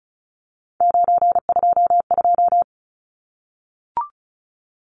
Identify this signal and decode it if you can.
Morse code, heard as a keyed tone, about 34 wpm, 700 Hz. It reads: 922
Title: Numbers - 3-Digit 35wpm